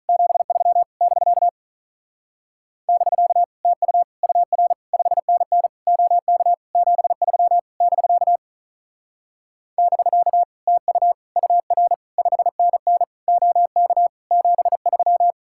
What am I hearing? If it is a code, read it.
Morse code, heard as a keyed tone, about 45 wpm, 700 Hz. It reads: 73 <BK> <BK> TU UR 5NN OK 73 <BK> <BK> TU UR 5NN OK 73